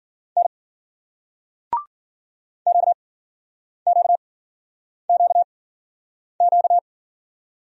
Morse code, heard as a keyed tone, about 52 wpm, spaced as for 9 wpm, 700 Hz. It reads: N QQQQ